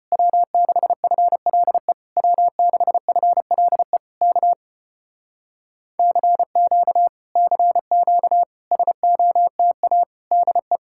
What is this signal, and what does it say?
Morse code, heard as a keyed tone, about 33 wpm, 700 Hz. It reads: W6FLE W6FLE K CQ CQ SOTA DE